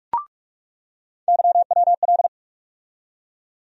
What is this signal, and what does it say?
Morse code, heard as a keyed tone, about 46 wpm, 700 Hz. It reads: YWL
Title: USA Call Sign Postfixes - 3 Letters 45wpm